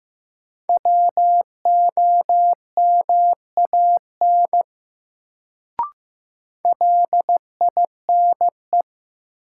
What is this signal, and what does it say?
Morse code, heard as a keyed tone, about 15 wpm, 700 Hz. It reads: WOMAN LINE
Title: Sets of 5 Words - New Word Drill - 15wpm